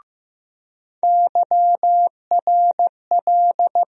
Morse code, heard as a keyed tone, about 15 wpm, 700 Hz. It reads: YRL
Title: USA Call Sign Postfixes - 3 Letters 15wpm